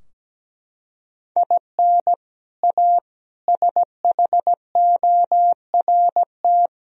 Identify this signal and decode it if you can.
Morse code, heard as a keyed tone, about 17 wpm, 700 Hz. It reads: IN A SHORT
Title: Sentences from Top 1000 Words - Review Part 2 - 17wpm